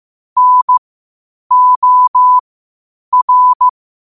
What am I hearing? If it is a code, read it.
Morse code, heard as a keyed tone, about 15 wpm, spaced as for 8 wpm, 1000 Hz. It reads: NOR
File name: qotdfiles / 8wpm